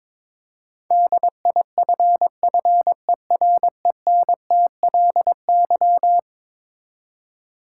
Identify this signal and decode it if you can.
Morse code, heard as a keyed tone, about 22 wpm, 700 Hz. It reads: DIFFERENTLY
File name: practice / 22wpm / Eleven Letter Words - 22wpm